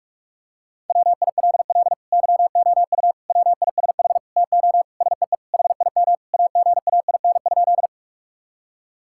Morse code, heard as a keyed tone, about 45 wpm, 700 Hz. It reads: WILL YOU WISH TO SEE HIM AGAIN?
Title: Sentences from Top 1600 Words 45wpm